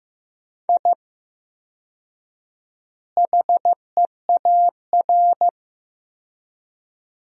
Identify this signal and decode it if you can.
Morse code, heard as a keyed tone, about 16 wpm, 700 Hz. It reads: I HEAR
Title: 4x Word-Spacing - Sentences from Top 500 Words - 15wpm